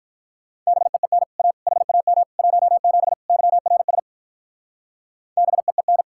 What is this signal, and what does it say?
Morse code, heard as a keyed tone, about 44 wpm, 700 Hz. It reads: BEEN A HAM 17 YRS BEEN
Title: Speed-Racing Common QSO Phrases - 35wpm